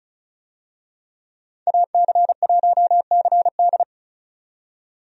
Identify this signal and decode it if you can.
Morse code, heard as a keyed tone, about 36 wpm, 700 Hz. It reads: AC1CD
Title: USA Call Signs - 5 Characters - 35wpm